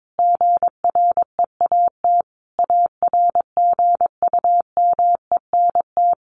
Morse code, heard as a keyed tone, about 22 wpm, 700 Hz. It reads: GREAT ARGUMENT
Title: Sentences from Top 1400 Words 22wpm